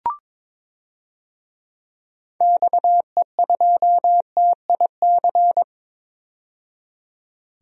Morse code, heard as a keyed tone, about 21 wpm, 700 Hz. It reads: XE2TIC
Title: DXCC-100 Call Signs - 6 Characters - 22wpm